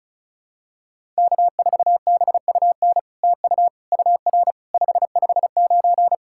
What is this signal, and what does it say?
Morse code, heard as a keyed tone, about 36 wpm, 700 Hz. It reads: K4BUN TU UR 559
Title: SOTA QSO Practice 35wpm